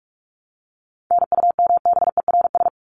Morse code, heard as a keyed tone, about 44 wpm, 700 Hz. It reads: NUMBERS
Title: ES:S Suffix - Top 100 Words with Base Words 45wpm